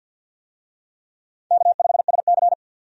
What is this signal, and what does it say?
Morse code, heard as a keyed tone, about 52 wpm, 700 Hz. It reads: K5SC
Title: USA Call Signs - 4 Characters - 50wpm